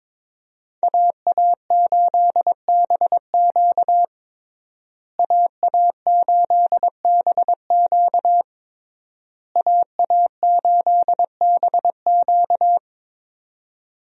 Morse code, heard as a keyed tone, about 22 wpm, 690 Hz. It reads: AA8BQ AA8BQ AA8BQ
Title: USA Call Signs - 5 Characters - Repeated 3x - 22wpm